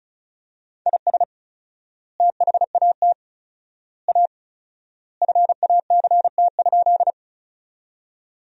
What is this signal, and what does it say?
Morse code, heard as a keyed tone, about 36 wpm, 700 Hz. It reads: IS THAT A FACT?
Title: 4x Word-Spacing - Sentences from Top 500 Words - 35wpm